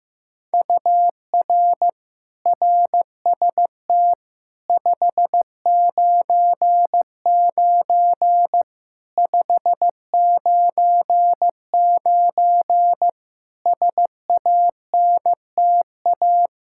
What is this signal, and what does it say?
Morse code, heard as a keyed tone, about 15 wpm, 700 Hz. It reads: UR RST 599 599 SANTA
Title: Practice QSO - Exchange 1 - Beginning - 15wpm